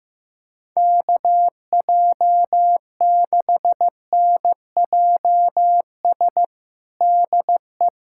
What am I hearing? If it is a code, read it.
Morse code, heard as a keyed tone, about 15 wpm, 700 Hz. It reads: KJ6NJS DE
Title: USA Callsign DE Callsign 15wpm